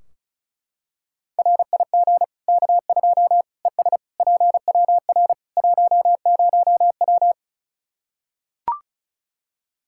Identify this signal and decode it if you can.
Morse code, heard as a keyed tone, about 36 wpm, 700 Hz. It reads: RIG K2 ES PWR 10W
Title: Practice QSO - Exchange 2 - Info - 35wpm